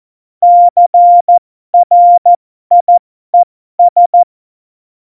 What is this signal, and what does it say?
Morse code, heard as a keyed tone, about 14 wpm, spaced as for 12 wpm, 700 Hz. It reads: CRIES
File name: Morsekurs / Witze